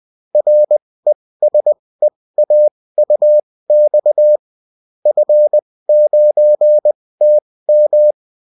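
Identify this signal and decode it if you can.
Morse code, heard as a keyed tone, about 20 wpm, spaced as for 16 wpm, 600 Hz. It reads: RESEAUX F9TM
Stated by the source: Bulletin in text and CW form.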